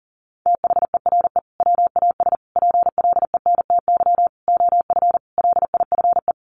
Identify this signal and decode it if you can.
Morse code, heard as a keyed tone, about 40 wpm, 700 Hz. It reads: THERE WAS PLENTY OF LIFE
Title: Sentences from Top 1300 Words 40wpm